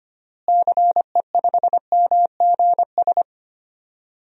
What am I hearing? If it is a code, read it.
Morse code, heard as a keyed tone, about 25 wpm, 700 Hz. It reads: CE5MGS
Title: DXCC-100 Call Signs - 6 Characters - Repeated 3x - 25wpm